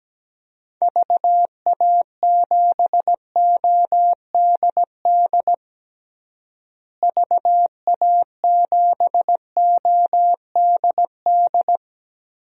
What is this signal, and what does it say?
Morse code, heard as a keyed tone, about 17 wpm, 700 Hz. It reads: VA7ODD VA7ODD
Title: DXCC-100 Call Signs - 6 Characters - Repeated 3x - 17wpm